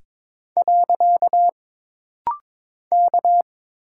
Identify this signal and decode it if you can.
Morse code, heard as a keyed tone, about 22 wpm, 700 Hz. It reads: . K
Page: This rapid-fire timing is geared to push your ICR (Instant Character Recognition) skills.